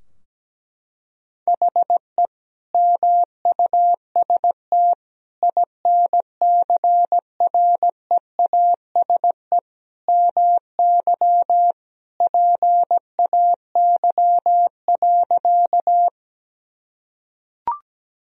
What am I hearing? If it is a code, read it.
Morse code, heard as a keyed tone, about 17 wpm, 700 Hz. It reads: HE MUST INCREASE MY PAY.
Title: Sentences from Top 1000 Words - Review Part 4 - 17wpm